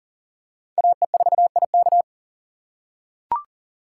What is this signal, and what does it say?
Morse code, heard as a keyed tone, about 40 wpm, 700 Hz. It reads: AE4IK
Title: USA Call Signs - 5 Characters - 4x Character Spacing - 40wpm